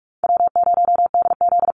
Morse code, heard as a keyed tone, about 44 wpm, 700 Hz. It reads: W0DZ
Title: USA Call Signs Repeated 2x 45wpm